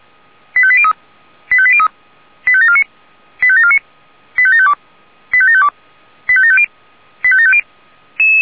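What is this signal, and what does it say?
Five-tone selective call (ZVEI), heard as quick runs of five tones, tones of about 70 ms. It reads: 85803 85803 85749 85749 85752 85752 85750 85750
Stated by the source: ZVEI five-tone call received by SDR.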